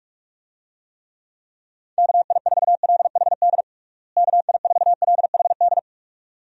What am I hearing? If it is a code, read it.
Morse code, heard as a keyed tone, about 46 wpm, 700 Hz. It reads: KI4LHD KI4LHD
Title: USA Call Signs - 6 Characters - Repeated 2x - 45wpm